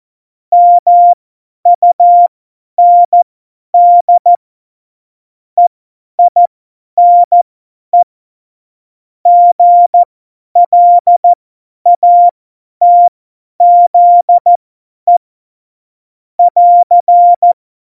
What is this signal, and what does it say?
Morse code, heard as a keyed tone, about 14 wpm, spaced as for 10 wpm, 700 Hz. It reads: MUND EINE GLATZE <AR>
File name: Morsekurs / Witze / Witz 24. Oktober 17, Tempo 50